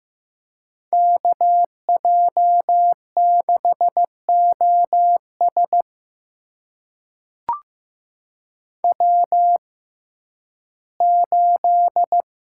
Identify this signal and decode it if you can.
Morse code, heard as a keyed tone, about 15 wpm, spaced as for 11 wpm, 700 Hz. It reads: KJ6OS W8
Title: USA Call Signs - 5 Characters - 6x Character Spacing - 15wpm